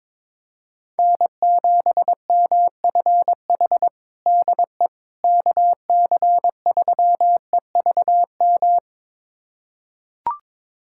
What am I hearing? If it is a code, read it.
Morse code, heard as a keyed tone, about 22 wpm, 700 Hz. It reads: N7MFH DE KC3EVM
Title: USA Callsign DE Callsign 22wpm